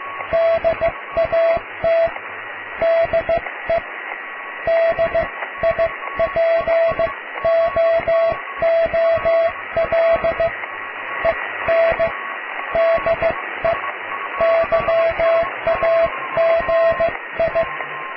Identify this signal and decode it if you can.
Morse code, heard as a keyed tone, about 15 wpm, spaced as for 13 wpm, 660 Hz, with radio noise. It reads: DAT DE DIPOOL EN DE YAGI
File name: cw / tekst(a)+QRM